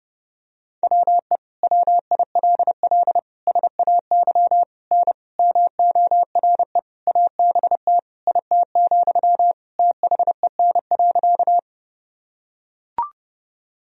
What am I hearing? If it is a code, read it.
Morse code, heard as a keyed tone, about 30 wpm, 700 Hz. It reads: WE WILL SAY N MORE ABT IT, THEN.
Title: Sentences from Top 200 Words and Abbreviations 30wpm